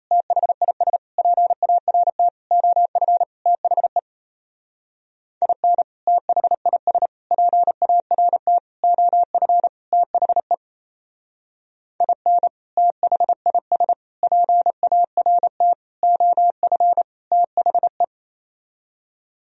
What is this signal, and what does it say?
Morse code, heard as a keyed tone, about 34 wpm, 700 Hz. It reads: THIS PART OF THE IN THIS PART OF THE IN THIS PART OF THE
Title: Speed-Racing - Sets of 5 Words 28wpm